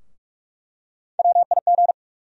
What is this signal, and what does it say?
Morse code, heard as a keyed tone, about 46 wpm, 700 Hz. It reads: WIG
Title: USA Call Sign Postfixes - 3 Letters 45wpm